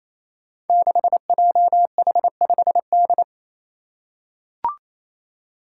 Morse code, heard as a keyed tone, about 27 wpm, 700 Hz. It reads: 6JH5D
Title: Five Challenging Letter-Number Combinations 28wpm